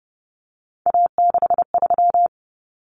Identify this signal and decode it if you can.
Morse code, heard as a keyed tone, about 30 wpm, 700 Hz. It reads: A63
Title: Common Cut Numbers - 3-Digit 30wpm